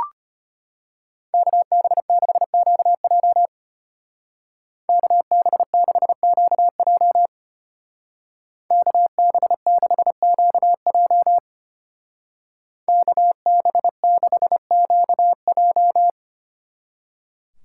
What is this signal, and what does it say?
Morse code, heard as a keyed tone, about 32 wpm, 700 Hz. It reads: KB6QJ KB6QJ KB6QJ KB6QJ